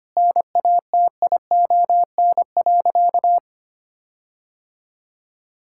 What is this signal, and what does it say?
Morse code, heard as a keyed tone, about 25 wpm, 700 Hz. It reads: NATION.
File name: practice / 25wpm / Sentences with 4-Word Phrases - 25wpm